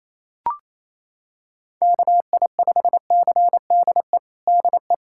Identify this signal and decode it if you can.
Morse code, heard as a keyed tone, about 28 wpm, 700 Hz. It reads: KI5CDE DE
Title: USA Callsign DE Callsign 28wpm